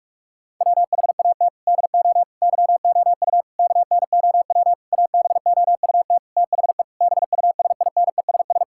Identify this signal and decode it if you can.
Morse code, heard as a keyed tone, about 41 wpm, 700 Hz. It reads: WHAT DO YOU KNOW ABOUT THE BUSINESS
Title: Sentences from Top 1000 Words - Review Part 6 - 45wpm